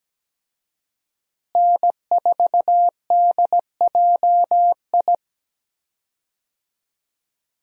Morse code, heard as a keyed tone, about 17 wpm, 700 Hz. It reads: N4DJI
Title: USA Call Signs - 5 Characters - Repeated 3x - 17wpm